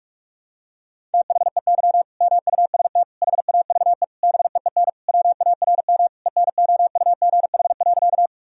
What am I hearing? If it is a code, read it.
Morse code, heard as a keyed tone, about 41 wpm, 700 Hz. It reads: THEY MUST HAVE BEEN WARM ENOUGH.
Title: Sentences from Top 1000 Words - Review Part 2 - 45wpm